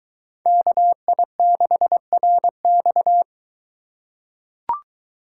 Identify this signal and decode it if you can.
Morse code, heard as a keyed tone, about 23 wpm, 700 Hz. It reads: KI6RX